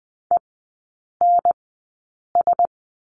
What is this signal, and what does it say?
Morse code, heard as a keyed tone, about 20 wpm, spaced as for 8 wpm, 700 Hz. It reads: ENS
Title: 035 Lesson 7e - Letter s - Three Characters 20wpm